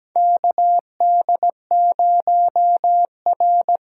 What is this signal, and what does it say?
Morse code, heard as a keyed tone, about 17 wpm, 700 Hz. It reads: KD0R
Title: USA Call Signs - 4 Characters - 2x Character Spacing - 17wpm